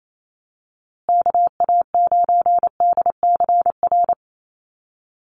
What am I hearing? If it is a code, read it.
Morse code, heard as a keyed tone, about 28 wpm, 700 Hz. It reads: KA9DCR
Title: USA Call Signs Repeated 2x 28wpm